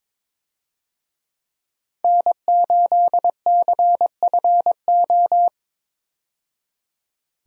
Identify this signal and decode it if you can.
Morse code, heard as a keyed tone, about 22 wpm, 700 Hz. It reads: N8CFO
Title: USA Call Signs - 5 Characters - 22wpm